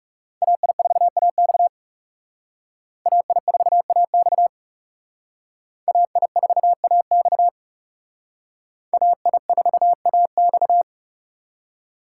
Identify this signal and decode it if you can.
Morse code, heard as a keyed tone, about 37 wpm, 700 Hz. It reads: AI4AX AI4AX AI4AX AI4AX
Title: Speed Racing - USA 2x2 Call Signs 30wpm